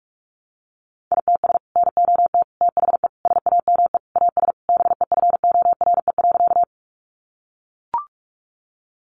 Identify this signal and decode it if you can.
Morse code, heard as a keyed tone, about 44 wpm, 700 Hz. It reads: ITS NOT THE SAME AS BEFORE.
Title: Sentences from Top 600 Words 45wpm